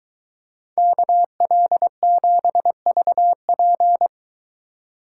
Morse code, heard as a keyed tone, about 23 wpm, 700 Hz. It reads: KL7VP